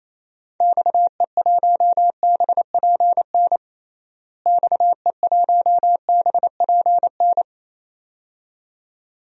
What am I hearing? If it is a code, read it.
Morse code, heard as a keyed tone, about 28 wpm, 700 Hz. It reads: XE1BPN XE1BPN
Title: DXCC-100 Call Signs - 6 Characters - Repeated 2x - 28wpm